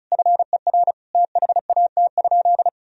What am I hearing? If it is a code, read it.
Morse code, heard as a keyed tone, about 34 wpm, 700 Hz. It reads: FER THAT?
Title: Sentences from Top 300 Words and Abbreviations 35wpm